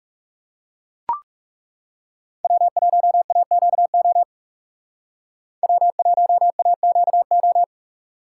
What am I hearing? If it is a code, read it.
Morse code, heard as a keyed tone, about 38 wpm, 700 Hz. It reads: W1AQO W1AQO